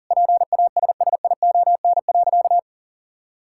Morse code, heard as a keyed tone, about 40 wpm, 700 Hz. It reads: PASSION.
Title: Sentences from Top 1600 Words 40wpm